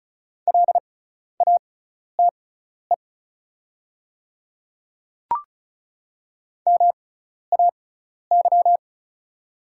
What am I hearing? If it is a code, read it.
Morse code, heard as a keyed tone, about 36 wpm, spaced as for 12 wpm, 700 Hz. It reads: LATE MAY
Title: 6x Character Spacing - Top 300 Words - 35wpm